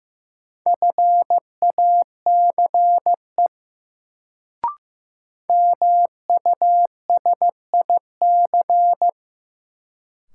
Morse code, heard as a keyed tone, about 15 wpm, 700 Hz. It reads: FACE MUSIC
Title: Sentences from Top 300 Words - New Word Drill - 15wpm